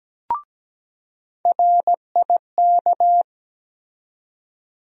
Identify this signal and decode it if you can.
Morse code, heard as a keyed tone, about 17 wpm, 700 Hz. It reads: RIK
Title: USA Call Sign Postfixes - 3 Letters 17wpm